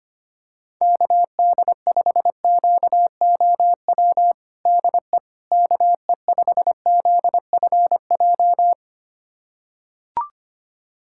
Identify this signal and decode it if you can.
Morse code, heard as a keyed tone, about 25 wpm, 700 Hz. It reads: KD5QOW DE KE5ZFJ
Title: USA Callsign DE Callsign 25wpm